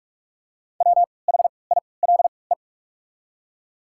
Morse code, heard as a keyed tone, about 42 wpm, spaced as for 25 wpm, 700 Hz. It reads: WHILE
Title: 3x Character Spacing - Top 300 Words - 45wpm